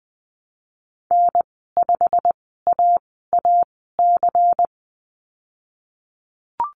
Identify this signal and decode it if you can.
Morse code, heard as a keyed tone, about 20 wpm, spaced as for 15 wpm, 700 Hz. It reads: N5AAC
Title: 122 Lesson 20g - Number 3 - Call Signs 20wpm